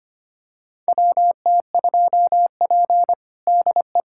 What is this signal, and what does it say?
Morse code, heard as a keyed tone, about 25 wpm, 700 Hz. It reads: WT2P DE
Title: USA Callsign DE Callsign - 4 Characters - 25wpm